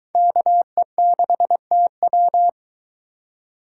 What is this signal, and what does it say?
Morse code, heard as a keyed tone, about 22 wpm, 700 Hz. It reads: KE6TW